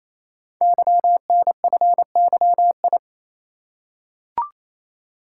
Morse code, heard as a keyed tone, about 29 wpm, 700 Hz. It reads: YNFYI
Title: Code Groups - 5 Letters - 28wpm